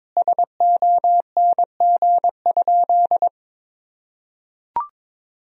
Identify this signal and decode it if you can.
Morse code, heard as a keyed tone, about 22 wpm, 700 Hz. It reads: SONG?
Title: Sentences from Top 1000 Words - Review Part 2 - 22wpm